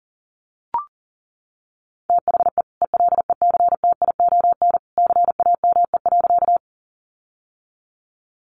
Morse code, heard as a keyed tone, about 37 wpm, 700 Hz. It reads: THE ELECTION CAME.
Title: Sentences from Top 1500 Words 40wpm